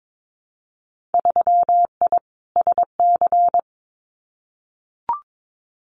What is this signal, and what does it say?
Morse code, heard as a keyed tone, about 22 wpm, 700 Hz. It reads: 3I SC
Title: Winter Field Day Exchanges 22wpm